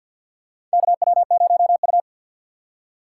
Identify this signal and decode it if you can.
Morse code, heard as a keyed tone, about 48 wpm, 700 Hz. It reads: KW0U
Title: USA Call Signs - 4 Characters - 6x Character Spacing - 50wpm